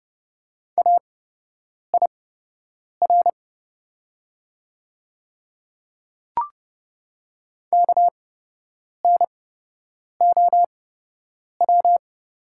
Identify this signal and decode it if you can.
Morse code, heard as a keyed tone, about 30 wpm, spaced as for 8 wpm, 700 Hz. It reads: AIR KNOW
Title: 8x Character Spacing - Top 200 Words - 30wpm